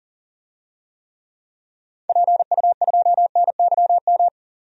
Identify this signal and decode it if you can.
Morse code, heard as a keyed tone, about 40 wpm, 700 Hz. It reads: PU2NYM
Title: DXCC-100 Call Signs - 6 Characters - 40wpm